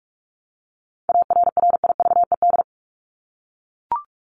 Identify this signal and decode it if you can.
Morse code, heard as a keyed tone, about 44 wpm, 700 Hz. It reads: ARRIVED
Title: ED Suffix - Top 100 Words with Base Words 45wpm